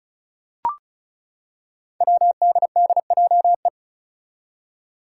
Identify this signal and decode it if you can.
Morse code, heard as a keyed tone, about 34 wpm, 700 Hz. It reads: WDDJE
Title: Five Challenging Letter-Number Combinations 35wpm